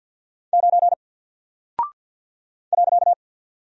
Morse code, heard as a keyed tone, about 48 wpm, 700 Hz. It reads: 9 .
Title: Single Character (Letter:Number:Punctuation:Prosign) 50wpm